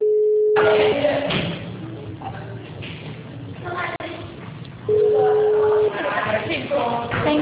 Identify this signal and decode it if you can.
Ringback tone (Europe, 425 Hz), with speech.
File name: out- -2015- - - .28244